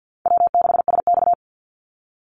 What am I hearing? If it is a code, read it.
Morse code, heard as a keyed tone, about 50 wpm, 700 Hz. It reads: W6SX
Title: Call Signs Repeated 3x - 50wpm